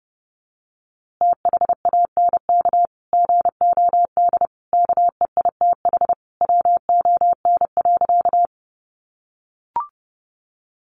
Morse code, heard as a keyed tone, about 30 wpm, 700 Hz. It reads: THANK GOD KEITH WON.
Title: A Sound Of Thunder 30wpm